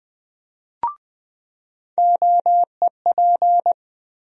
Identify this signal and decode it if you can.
Morse code, heard as a keyed tone, about 20 wpm, 700 Hz. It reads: OEP
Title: USA Call Sign Postfixes - 3 Letters 20wpm